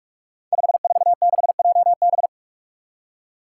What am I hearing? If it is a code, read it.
Morse code, heard as a keyed tone, about 42 wpm, 700 Hz. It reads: 546JB
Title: Five Challenging Letter-Number Combinations 45wpm